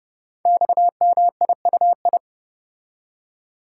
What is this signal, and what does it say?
Morse code, heard as a keyed tone, about 30 wpm, 700 Hz. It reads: XMIUI
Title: Code Groups - 5 Letters - 30wpm